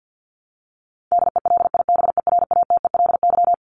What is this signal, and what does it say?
Morse code, heard as a keyed tone, about 50 wpm, 700 Hz. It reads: DELIBERATELY
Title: Top 1751-2000 Words 50wpm